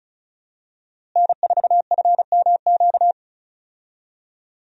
Morse code, heard as a keyed tone, about 36 wpm, 700 Hz. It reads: N4FMQ
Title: USA Call Signs - 5 Characters - Repeated 3x - 35wpm